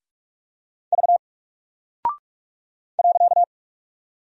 Morse code, heard as a keyed tone, about 44 wpm, 700 Hz. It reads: V .
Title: Single Character (Letter:Number:Punctuation:Prosign) Alt 45wpm